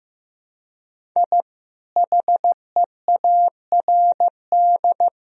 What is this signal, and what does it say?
Morse code, heard as a keyed tone, about 16 wpm, 700 Hz. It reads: I HEARD
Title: Sentences from Top 1000 Words - Review Part 1 - 15wpm